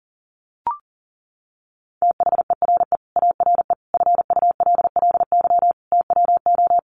Morse code, heard as a keyed tone, about 40 wpm, 700 Hz. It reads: THERE ARE FULLY TWO
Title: Sentences from Top 1300 Words 40wpm